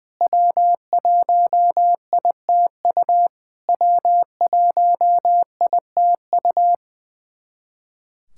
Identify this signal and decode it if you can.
Morse code, heard as a keyed tone, about 21 wpm, 690 Hz. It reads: W1ITU W1ITU
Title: SOTA QSO Practice 20wpm